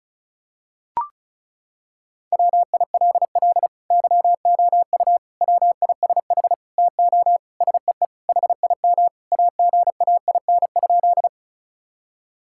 Morse code, heard as a keyed tone, about 34 wpm, 700 Hz. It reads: WILL YOU WISH TO SEE HIM AGAIN?
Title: Sentences from Top 1600 Words 35wpm